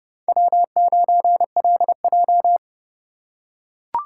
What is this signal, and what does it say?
Morse code, heard as a keyed tone, about 27 wpm, 700 Hz. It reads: W9LJ